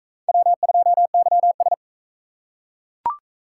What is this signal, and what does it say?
Morse code, heard as a keyed tone, about 39 wpm, 700 Hz. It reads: W2YS